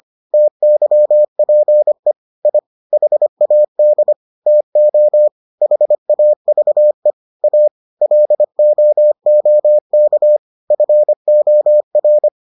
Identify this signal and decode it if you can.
Morse code, heard as a keyed tone, about 25 wpm, 600 Hz. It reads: TYPE I HAD TO HAVE A LOOK FOR